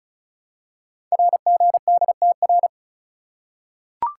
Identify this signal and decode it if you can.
Morse code, heard as a keyed tone, about 36 wpm, 690 Hz. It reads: RGDTR
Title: Code Groups - 5 Letters - 35wpm